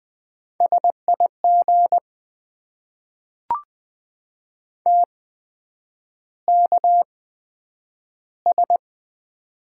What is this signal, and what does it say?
Morse code, heard as a keyed tone, about 20 wpm, spaced as for 9 wpm, 700 Hz. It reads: SIG TKS
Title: Top 90 QSO Elements - 8x Character Spacing - 20wpm